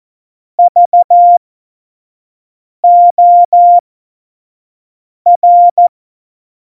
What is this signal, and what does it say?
Morse code, heard as a keyed tone, about 14 wpm, spaced as for 5 wpm, 700 Hz. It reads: VOR
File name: Morsekurs / Witze / Witz 28. Januar 2020, 25 F70